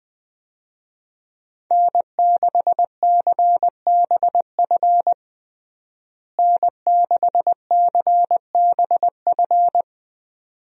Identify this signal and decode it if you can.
Morse code, heard as a keyed tone, about 20 wpm, 700 Hz. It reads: N6CBF N6CBF
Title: USA Call Signs - 5 Characters - Repeated 2x - 20wpm